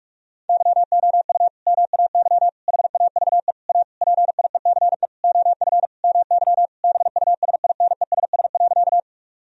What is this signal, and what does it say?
Morse code, heard as a keyed tone, about 46 wpm, 700 Hz. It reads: YOU MAY HAVE A PIECE OF MY BUSINESS.
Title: Sentences from Top 1000 Words - Review Part 2 - 45wpm